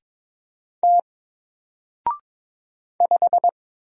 Morse code, heard as a keyed tone, about 22 wpm, 700 Hz. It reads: T 5
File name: Single-Letter-Number-22wpm